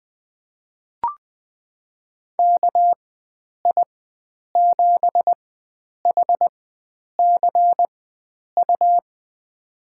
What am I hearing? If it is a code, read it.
Morse code, heard as a keyed tone, about 20 wpm, spaced as for 9 wpm, 700 Hz. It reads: KI7HCU
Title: USA Call Signs - 6 Characters - 4x Character Spacing - 20wpm